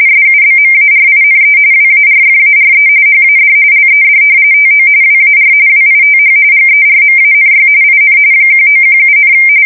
RTTY, about 45.45 baud. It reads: SUBMARINE WARFARE HAD LED TO WAR WITH GERMANY IN 1917,
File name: fsk170 / usnavy4